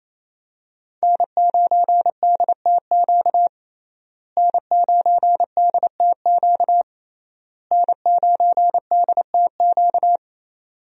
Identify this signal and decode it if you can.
Morse code, heard as a keyed tone, about 28 wpm, 700 Hz. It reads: N9DTQ N9DTQ N9DTQ
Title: USA Call Signs - 5 Characters - Repeated 3x - 28wpm